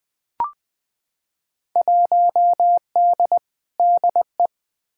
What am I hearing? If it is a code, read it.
Morse code, heard as a keyed tone, about 20 wpm, 700 Hz. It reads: 1D DE
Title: Field Day Exchanges - 20wpm